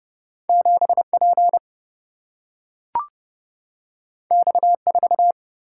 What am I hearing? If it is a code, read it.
Morse code, heard as a keyed tone, about 30 wpm, 700 Hz. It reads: 7 P X 4
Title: Single-Letter Single-Number 30wpm